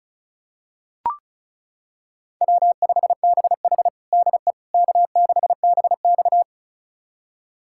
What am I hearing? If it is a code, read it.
Morse code, heard as a keyed tone, about 34 wpm, 700 Hz. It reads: W5BH DE K6BX
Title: USA Callsign DE Callsign - 4 Characters - 35wpm